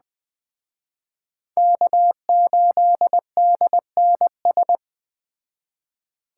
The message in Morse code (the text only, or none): K8DNS